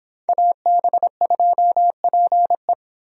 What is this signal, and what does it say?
Morse code, heard as a keyed tone, about 26 wpm, 700 Hz. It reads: AB2PE